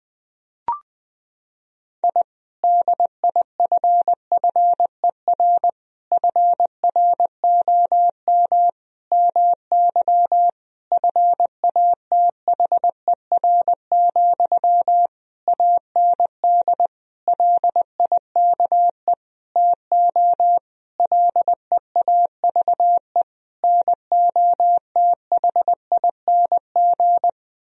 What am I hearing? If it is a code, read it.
Morse code, heard as a keyed tone, about 20 wpm, 700 Hz. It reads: I DIFFER FROM MY FATHER, AND LIKE TO LEAVE NOTHING
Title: Sentences from Top 1000 Words - Review Part 1 - 20wpm